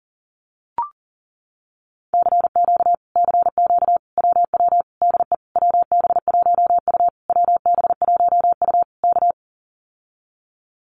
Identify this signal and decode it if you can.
Morse code, heard as a keyed tone, about 38 wpm, 700 Hz. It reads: CQ CQ WW DE WB1U WB1U K
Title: Common QSO Phrases Encore 40wpm